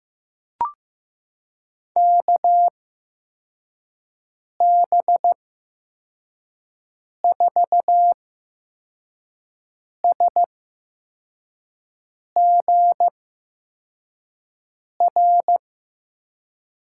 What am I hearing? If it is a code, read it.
Morse code, heard as a keyed tone, about 16 wpm, spaced as for 4 wpm, 700 Hz. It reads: KB4SGR
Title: USA Call Signs - 8x Character Spacing - 15wpm